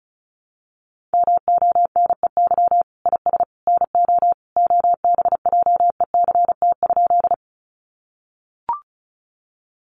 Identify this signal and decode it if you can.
Morse code, heard as a keyed tone, about 34 wpm, 700 Hz. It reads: MONEY IS NO OBJECT?
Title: Sentences from Top 500 Words 35wpm